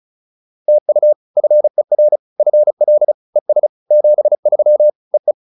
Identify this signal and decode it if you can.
Morse code, heard as a keyed tone, about 35 wpm, 600 Hz. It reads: TU FER FL ES 73 EE